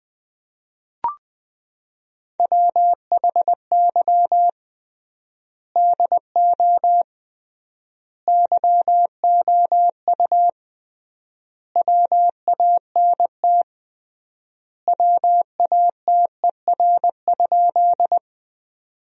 Morse code, heard as a keyed tone, about 20 wpm, 700 Hz. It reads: WHY DO YOU WANT WATER?
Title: 3x Word-Spacing - Sentences from Top 200 Words - 20wpm